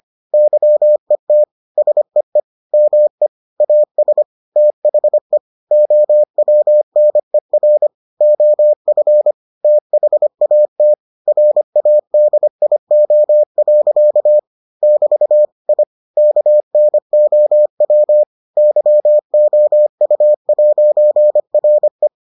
Morse code, heard as a keyed tone, about 25 wpm, 600 Hz. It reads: YET SEE ME AS THE OWNER OF THAT RADIO. <BT> I KNOW YOU'RE